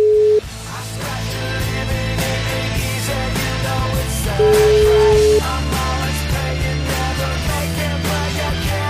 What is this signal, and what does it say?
Ringback tone (Europe, 425 Hz), with noise.